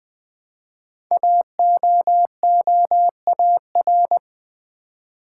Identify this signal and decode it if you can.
Morse code, heard as a keyed tone, about 20 wpm, 700 Hz. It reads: AOOAR
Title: Code Groups - 5 Letter:Numbers - 20wpm